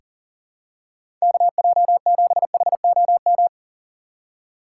Morse code, heard as a keyed tone, about 40 wpm, 700 Hz. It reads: KJ7HOM
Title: USA Call Signs - 6x Character Spacing - 40wpm